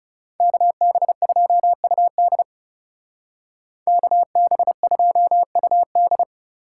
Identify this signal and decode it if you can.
Morse code, heard as a keyed tone, about 38 wpm, 700 Hz. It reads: KB2UD KB2UD